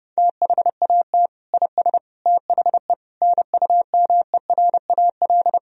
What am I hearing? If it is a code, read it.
Morse code, heard as a keyed tone, about 32 wpm, 700 Hz. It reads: THAT IS THE NUMERAL
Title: Sentences from Top 1000 Words - Review Part 2 - 30wpm